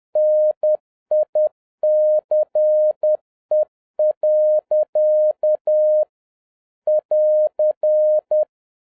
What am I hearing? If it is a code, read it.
Morse code, heard as a keyed tone, about 10 wpm, 610 Hz. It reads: NICE. <AR>
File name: 2011-07-mp3 / n56 BACK FROM A BIG ACCIDENT 10wpm